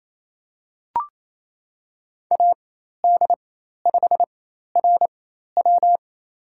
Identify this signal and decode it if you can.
Morse code, heard as a keyed tone, about 28 wpm, spaced as for 13 wpm, 700 Hz. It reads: AD5RW
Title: USA Call Signs - 5 Characters - 4x Character Spacing - 28wpm